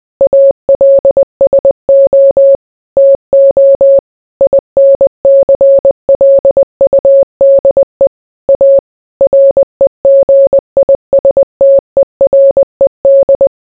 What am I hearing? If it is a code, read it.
Morse code, heard as a keyed tone, about 20 wpm, 550 Hz. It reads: ALSO TO INCLUDE A REGISTERED